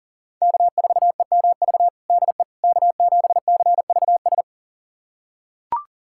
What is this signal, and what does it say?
Morse code, heard as a keyed tone, about 40 wpm, 700 Hz. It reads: K4EMV DE K7CVS
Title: USA Callsign DE Callsign - 5 Characters - 40wpm